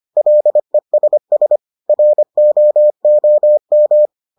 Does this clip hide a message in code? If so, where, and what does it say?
Yes: Morse code from 0.2 to 4.1 s, heard as a keyed tone, about 25 wpm, 600 Hz. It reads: LESS ROOM